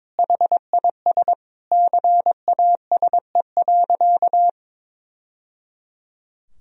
Morse code, heard as a keyed tone, about 22 wpm, 700 Hz. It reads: HIS CASE.
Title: Sentences from Top 1000 Words - Review Part 2 - 22wpm